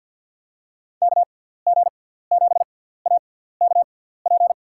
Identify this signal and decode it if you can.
Morse code, heard as a keyed tone, about 52 wpm, spaced as for 17 wpm, 700 Hz. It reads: KG7AKP
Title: USA Call Signs - 6 Characters - 6x Character Spacing - 50wpm